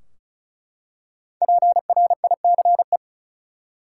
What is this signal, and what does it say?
Morse code, heard as a keyed tone, about 36 wpm, 700 Hz. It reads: PRICE
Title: Five Letter Words - 35wpm